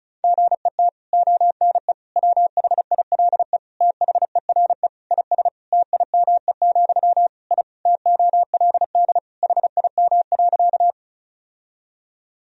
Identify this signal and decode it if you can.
Morse code, heard as a keyed tone, about 36 wpm, 700 Hz. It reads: GET ONE WHILE THERE IS TIME, I TOLD HIM.
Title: Sentences from Top 1000 Words - Review Part 1 - 35wpm